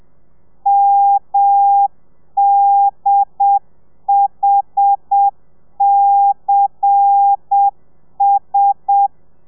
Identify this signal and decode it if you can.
Morse code, heard as a keyed tone, about 7 wpm, 800 Hz. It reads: MDHCS
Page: Lezione 1: audio a 30 CPM